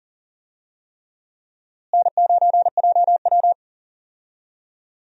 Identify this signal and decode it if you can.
Morse code, heard as a keyed tone, about 40 wpm, 700 Hz. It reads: N9JW
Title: USA Call Signs - 4 Characters - Repeated 3x - 40wpm